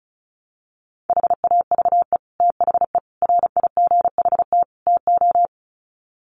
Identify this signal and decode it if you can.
Morse code, heard as a keyed tone, about 34 wpm, 700 Hz. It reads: HAVE THE RIGHT TO
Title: Sets of 4 Words 35wpm